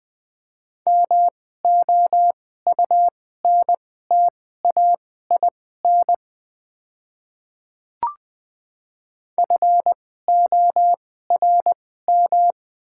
Morse code, heard as a keyed tone, about 20 wpm, spaced as for 15 wpm, 690 Hz. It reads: MOUNTAIN FORM
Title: 2x Character Spacing - Top 300 Words - 20wpm